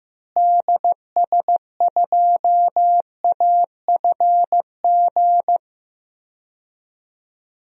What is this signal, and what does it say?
Morse code, heard as a keyed tone, about 16 wpm, 700 Hz. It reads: DS2AFG
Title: DXCC-100 Call Signs - 6 Characters - Repeated 2x - 15wpm